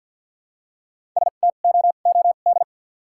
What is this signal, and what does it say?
Morse code, heard as a keyed tone, about 49 wpm, spaced as for 36 wpm, 700 Hz. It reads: STOOD
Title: 2x Character Spacing - Top 500 Words - 50wpm